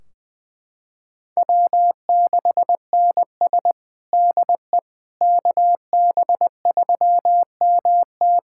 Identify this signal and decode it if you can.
Morse code, heard as a keyed tone, about 20 wpm, 700 Hz. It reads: W6NS DE KB3MT
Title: Practice QSO - Exchange 4 - Ending - 20wpm